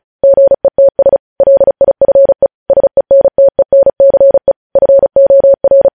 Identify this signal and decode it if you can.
Morse code, heard as a keyed tone, about 36 wpm, 550 Hz. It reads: GETS LIFE SENTENCE FOR
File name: podcast / News headlines-1x-35x35